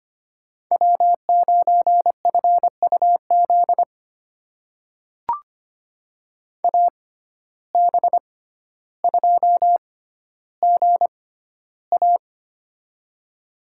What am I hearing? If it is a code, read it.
Morse code, heard as a keyed tone, about 25 wpm, spaced as for 13 wpm, 700 Hz. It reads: W9FUZ AB2GA
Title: USA Call Signs - 5 Characters - 6x Character Spacing - 25wpm